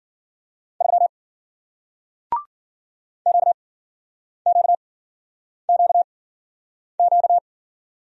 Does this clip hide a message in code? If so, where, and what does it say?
Yes: Morse code from 0.8 to 7.4 s, heard as a keyed tone, about 56 wpm, spaced as for 9 wpm, 700 Hz. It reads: 3 QQQQ